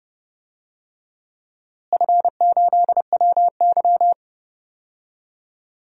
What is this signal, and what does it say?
Morse code, heard as a keyed tone, about 30 wpm, 700 Hz. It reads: F8WY
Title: DXCC-100 Call Signs - 4 Characters - Repeated 3x - 30wpm